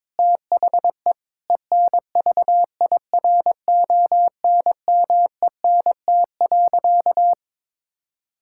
Morse code, heard as a keyed tone, about 22 wpm, 700 Hz. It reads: THE ENVIRONMENT.
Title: Sentences from Top 1000 Words - Review Part 6 - 22wpm